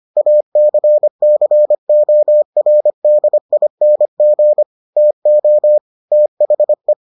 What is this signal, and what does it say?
Morse code, heard as a keyed tone, about 25 wpm, 600 Hz. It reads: ACCORDING TO THE